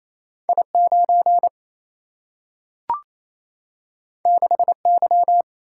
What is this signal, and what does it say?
Morse code, heard as a keyed tone, about 28 wpm, 700 Hz. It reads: I9 6Y
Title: Single-Letter Single-Number 28wpm